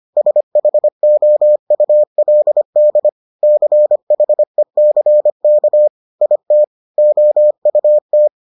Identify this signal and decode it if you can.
Morse code, heard as a keyed tone, about 25 wpm, 600 Hz. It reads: SHOULD CHECK IT OUT